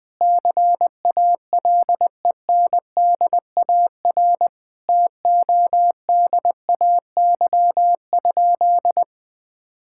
Morse code, heard as a keyed tone, about 20 wpm, 700 Hz. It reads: CALENDAR TODAY?
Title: Sentences from Top 1600 Words 20wpm